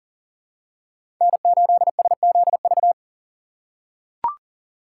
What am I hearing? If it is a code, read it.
Morse code, heard as a keyed tone, about 40 wpm, 700 Hz. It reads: N8SZV
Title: USA Call Signs - 6x Character Spacing - 40wpm